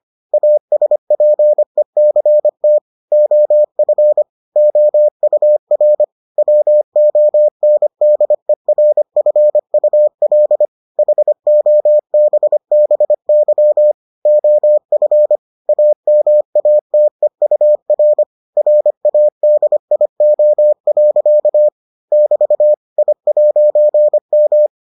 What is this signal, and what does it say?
Morse code, heard as a keyed tone, about 25 wpm, 600 Hz. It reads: ASPECT OF OUR WONDERFUL HOBBY OF AMATEUR RADIO. <BT> I'M